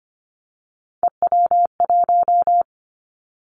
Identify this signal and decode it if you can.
Morse code, heard as a keyed tone, about 25 wpm, 700 Hz. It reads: EW1
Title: Callsign Prefixes for Top 50 DXCC - 25wpm